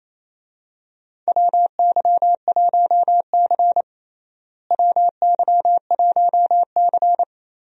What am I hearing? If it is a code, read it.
Morse code, heard as a keyed tone, about 27 wpm, 700 Hz. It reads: WY1C WY1C
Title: USA Call Signs - 4 Characters - Repeated 3x - 28wpm